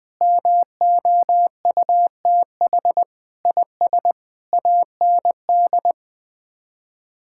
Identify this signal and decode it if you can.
Morse code, heard as a keyed tone, about 20 wpm, 700 Hz. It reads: MOUTH IS AND
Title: Sets of 8 Words - 20wpm